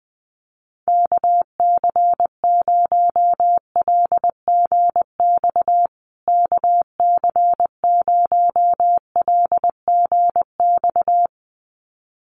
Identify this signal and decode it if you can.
Morse code, heard as a keyed tone, about 20 wpm, 700 Hz. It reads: KC0LGX KC0LGX
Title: USA Call Signs Repeated 2x 20wpm